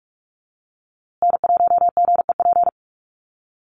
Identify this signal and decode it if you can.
Morse code, heard as a keyed tone, about 44 wpm, 700 Hz. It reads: N1GEP
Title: USA Call Signs Repeated 3x 45wpm